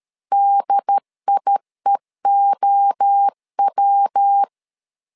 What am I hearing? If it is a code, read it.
Morse code, heard as a keyed tone, about 13 wpm, 800 Hz. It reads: DIEOW
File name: LEZIONI / 3 MWRO / 3 MWRO 12wpm 1730